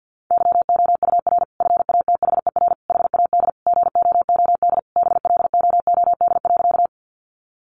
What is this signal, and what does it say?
Morse code, heard as a keyed tone, about 50 wpm, 700 Hz. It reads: YOUR FATHER HAD GOOD BLOOD.
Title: Sentences from Top 1100 Words 50wpm